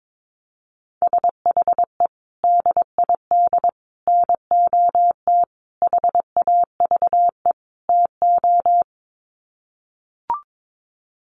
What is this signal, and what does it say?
Morse code, heard as a keyed tone, about 22 wpm, 700 Hz. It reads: SHE DID NOT HAVE TO
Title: Sets of 5 Words Encore2 22wpm